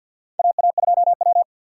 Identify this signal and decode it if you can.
Morse code, heard as a keyed tone, about 51 wpm, 700 Hz. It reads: AA2W
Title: USA Call Signs - 4 Characters - 6x Character Spacing - 50wpm